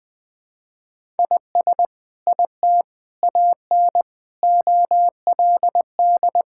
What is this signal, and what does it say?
Morse code, heard as a keyed tone, about 20 wpm, 700 Hz. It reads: IS IT AN OLD
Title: Sentences from Top 1000 Words - Review Part 5 - 20wpm